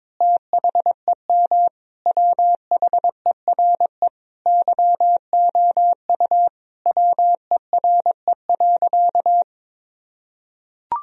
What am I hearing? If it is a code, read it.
Morse code, heard as a keyed tone, about 22 wpm, 700 Hz. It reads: THEM WHERE YOU WERE.
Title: Sentences from Top 1000 Words - Review Part 4 - 22wpm